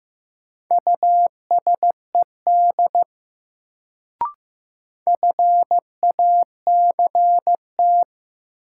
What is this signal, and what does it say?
Morse code, heard as a keyed tone, about 15 wpm, 700 Hz. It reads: USED FACT
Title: Sets of 2 Words - New Word Drill - 15wpm